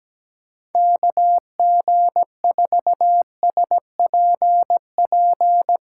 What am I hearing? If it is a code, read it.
Morse code, heard as a keyed tone, about 17 wpm, 700 Hz. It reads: KG4SPP
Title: USA Callsign DE Callsign - 6 Characters - 17wpm